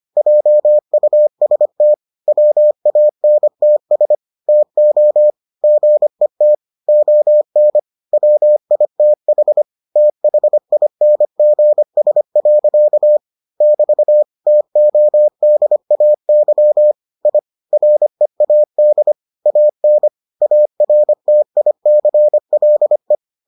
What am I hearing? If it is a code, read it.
Morse code, heard as a keyed tone, about 25 wpm, 600 Hz. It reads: JUST WANTS TO GET ON WITH THINGS. <BT> TODAY I READ AN ARTICLE